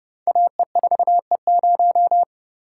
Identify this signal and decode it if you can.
Morse code, heard as a keyed tone, about 30 wpm, 700 Hz. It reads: AE4E0
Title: Code Groups - 5 Letter:Numbers - 30wpm